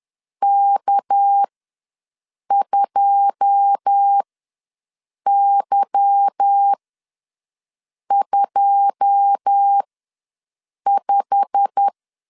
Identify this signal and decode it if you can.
Morse code, heard as a keyed tone, about 10 wpm, spaced as for 6 wpm, 800 Hz. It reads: K2Y25
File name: LEZIONI / 9 2 ws 10wpm 4130